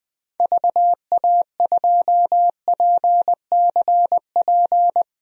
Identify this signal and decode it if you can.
Morse code, heard as a keyed tone, about 20 wpm, 700 Hz. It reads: VA2PCP
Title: DXCC-100 Call Signs - 6 Characters - Repeated 3x - 20wpm